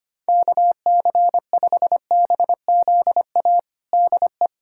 Morse code, heard as a keyed tone, about 25 wpm, 700 Hz. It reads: KC5BZA DE
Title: USA Callsign DE Callsign 25wpm